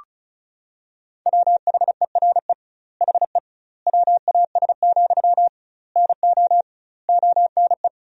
Morse code, heard as a keyed tone, about 36 wpm, 700 Hz. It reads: WHERE HE WAS, NO ONE
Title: 2x Word-Spacing - Sentences from Top 400 Words - 35wpm